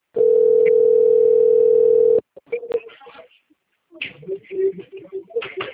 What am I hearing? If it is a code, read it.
Ringback tone (North America), 440 + 480 Hz, with speech.
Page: Guarapuava - PR